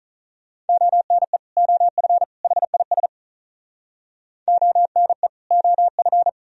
Morse code, heard as a keyed tone, about 46 wpm, 700 Hz. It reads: ONE OF HIS ONE OF